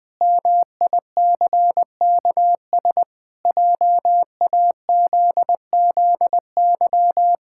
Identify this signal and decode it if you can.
Morse code, heard as a keyed tone, about 20 wpm, 700 Hz. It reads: MICKS JAZZY
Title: Pangram Challenge v2 - 20wpm